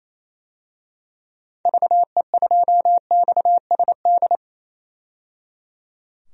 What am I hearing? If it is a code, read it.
Morse code, heard as a keyed tone, about 28 wpm, 700 Hz. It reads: VE2XSD
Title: DXCC-100 Call Signs - 6 Characters - 28wpm